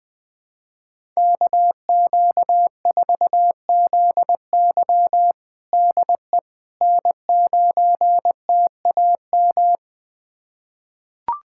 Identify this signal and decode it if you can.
Morse code, heard as a keyed tone, about 20 wpm, 700 Hz. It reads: KQ4ZY DE N9TAM
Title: USA Callsign DE Callsign - 5 Characters - 20wpm